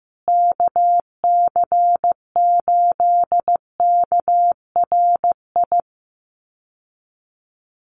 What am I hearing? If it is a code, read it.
Morse code, heard as a keyed tone, about 15 wpm, 700 Hz. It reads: KC8KRI
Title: USA Call Signs Repeated 3x 15wpm